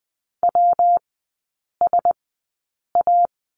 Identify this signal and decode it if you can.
Morse code, heard as a keyed tone, about 20 wpm, spaced as for 8 wpm, 700 Hz. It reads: WSA
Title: 133 Lesson 22e - Question Mark - Three Characters 20wpm